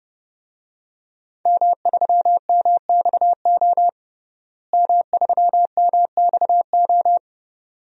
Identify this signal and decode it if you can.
Morse code, heard as a keyed tone, about 32 wpm, 700 Hz. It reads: M3MXO M3MXO
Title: DXCC-100 Call Signs - 5 Characters - Repeated 2x - 30wpm